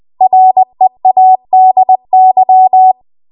Morse code, heard as a keyed tone, about 20 wpm, 750 Hz. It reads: READY
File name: Archive / 20 WPM / 20WPM